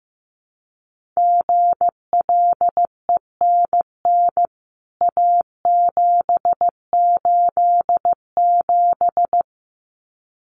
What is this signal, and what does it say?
Morse code, heard as a keyed tone, about 15 wpm, 700 Hz. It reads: GLENN A787
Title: CWT Exchange 15wpm